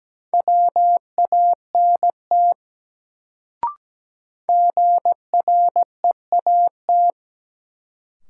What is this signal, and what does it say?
Morse code, heard as a keyed tone, about 17 wpm, 700 Hz. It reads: WANT GREAT
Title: Sentences from Top 200 Words - New Word Drill - 17wpm